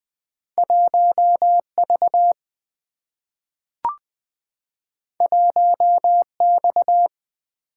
Morse code, heard as a keyed tone, about 20 wpm, 700 Hz. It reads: 1V 1X
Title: Single-Letter Single-Number 20wpm